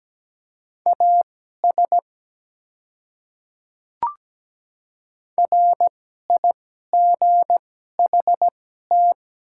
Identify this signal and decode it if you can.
Morse code, heard as a keyed tone, about 17 wpm, spaced as for 12 wpm, 700 Hz. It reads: AS RIGHT
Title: 2x Character Spacing - Top 200 Words - 17wpm